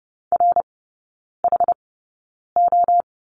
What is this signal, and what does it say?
Morse code, heard as a keyed tone, about 30 wpm, spaced as for 9 wpm, 700 Hz. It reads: RHO
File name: course / 30:10wpm / 087 Lesson 15e - Number 5 - Three Characters 30wpm